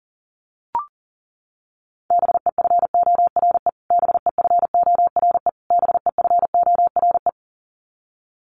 Morse code, heard as a keyed tone, about 40 wpm, 700 Hz. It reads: BEFORE BEFORE BEFORE
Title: Top 500 Words Repeated 40wpm